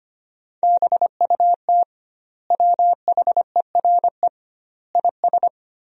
Morse code, heard as a keyed tone, about 25 wpm, 700 Hz. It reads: BUT WHERE IS
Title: 2x Word-Spacing - Sentences from Top 200 Words - 25wpm